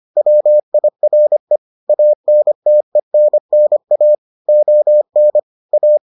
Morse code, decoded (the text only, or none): WIRE ANTENNA ON A